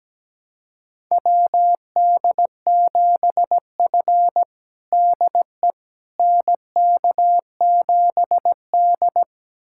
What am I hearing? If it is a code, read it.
Morse code, heard as a keyed tone, about 17 wpm, 700 Hz. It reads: WD7F DE NK7D
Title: USA Callsign DE Callsign - 4 Characters - 17wpm